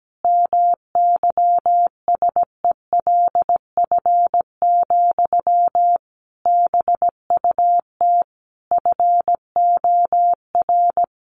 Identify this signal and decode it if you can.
Morse code, heard as a keyed tone, about 17 wpm, 700 Hz. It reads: MYSELF, BUT FOR
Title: Sentences from Top 1300 Words 17wpm